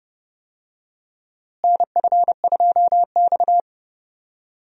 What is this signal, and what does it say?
Morse code, heard as a keyed tone, about 32 wpm, 700 Hz. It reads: NF2X
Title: USA Call Signs - 4 Characters - Repeated 2x - 30wpm